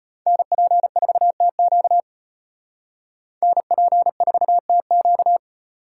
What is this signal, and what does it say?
Morse code, heard as a keyed tone, about 30 wpm, 700 Hz. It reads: NP4TQ NP4TQ
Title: Speed Racing - USA 2x2 Call Signs 25wpm